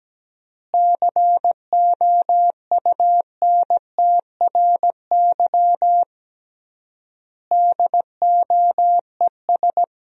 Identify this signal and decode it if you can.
Morse code, heard as a keyed tone, about 17 wpm, 700 Hz. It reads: COUNTRY DOES
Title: 3x Word-Spacing - Sentences from Top 300 Words - 17wpm